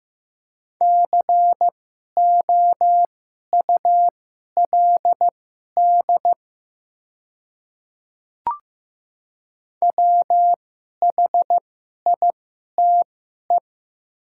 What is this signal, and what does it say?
Morse code, heard as a keyed tone, about 15 wpm, spaced as for 11 wpm, 700 Hz. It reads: COULD WHITE
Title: 2x Character Spacing - Top 300 Words - 15wpm